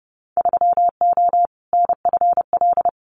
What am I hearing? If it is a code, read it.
Morse code, heard as a keyed tone, about 30 wpm, 700 Hz. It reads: 3O NFL
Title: Winter Field Day Exchanges 30wpm